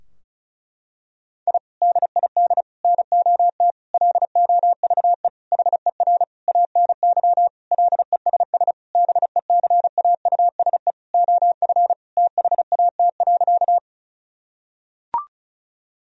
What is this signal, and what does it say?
Morse code, heard as a keyed tone, about 36 wpm, 700 Hz. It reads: I DID NOT LOVE HER ANY LESS BECAUSE OF THAT.
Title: Sentences from Top 1000 Words - Review Part 2 - 35wpm